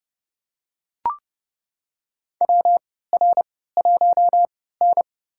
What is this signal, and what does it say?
Morse code, heard as a keyed tone, about 30 wpm, spaced as for 17 wpm, 700 Hz. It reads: WR1N
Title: USA Call Signs - 4 Characters - 3x Character Spacing - 30wpm